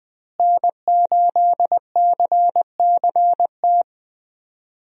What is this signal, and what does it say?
Morse code, heard as a keyed tone, about 20 wpm, 700 Hz. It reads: N8CCT
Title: USA Call Signs - 5 Characters - Repeated 3x - 20wpm